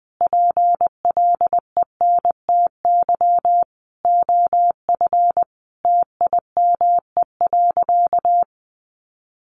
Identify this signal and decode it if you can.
Morse code, heard as a keyed tone, about 20 wpm, 700 Hz. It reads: PLENTY OF TIME.
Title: Sentences from Top 1300 Words 20wpm